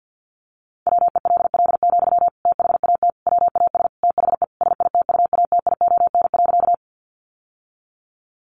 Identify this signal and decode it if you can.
Morse code, heard as a keyed tone, about 50 wpm, 700 Hz. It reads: WELL, THAT WAS THE SITUATION.
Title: Sentences from Top 1200 Words 50wpm